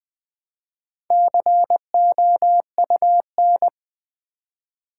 Morse code, heard as a keyed tone, about 20 wpm, 700 Hz. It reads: COUN
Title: Build A Word - 4 Letters At A Time - 11-Letter Words - 20wpm